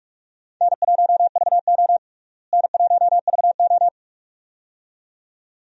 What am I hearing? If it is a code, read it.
Morse code, heard as a keyed tone, about 46 wpm, 700 Hz. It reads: N1VO N1VO
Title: USA Call Signs - 4 Characters - Repeated 2x - 45wpm